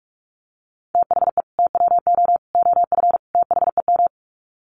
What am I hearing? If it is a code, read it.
Morse code, heard as a keyed tone, about 45 wpm, 700 Hz. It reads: THE TWO OF THEM
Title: Sets of 4 Words 45wpm